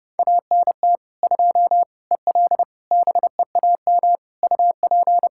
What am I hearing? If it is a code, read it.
Morse code, heard as a keyed tone, about 32 wpm, 700 Hz. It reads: ANT 2 EL BEAM UP
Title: Practice QSO - Exchange 2 - Info - 30wpm